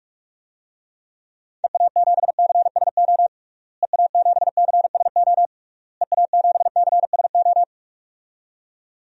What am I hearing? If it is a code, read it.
Morse code, heard as a keyed tone, about 46 wpm, 700 Hz. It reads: EA7CSO EA7CSO EA7CSO
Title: DXCC-100 Call Signs - 6 Characters - Repeated 3x - 45wpm